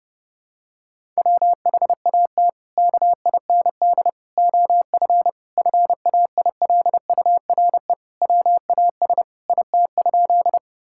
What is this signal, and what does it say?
Morse code, heard as a keyed tone, about 30 wpm, 700 Hz. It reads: WHAT KIND OF FAILURE WAS IT?
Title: Sentences from Top 1000 Words - Review Part 6 - 30wpm